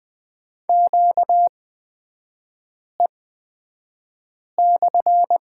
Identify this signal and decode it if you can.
Morse code, heard as a keyed tone, about 20 wpm, spaced as for 5 wpm, 700 Hz. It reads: QE/
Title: 243 Lesson 39e - Letter z - Three Characters 20wpm